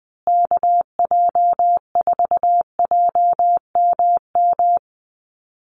Morse code, heard as a keyed tone, about 20 wpm, 700 Hz. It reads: KJ4JMM
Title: USA Call Signs Repeated 3x 20wpm